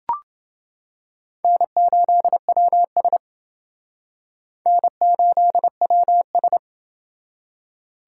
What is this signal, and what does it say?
Morse code, heard as a keyed tone, about 28 wpm, 700 Hz. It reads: N8WS N8WS